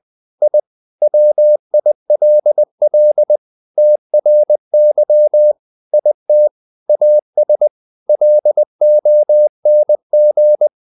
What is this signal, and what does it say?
Morse code, heard as a keyed tone, about 20 wpm, 600 Hz. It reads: I WILL TRY IT AS LONG
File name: the old man and the sea part 2 20wpm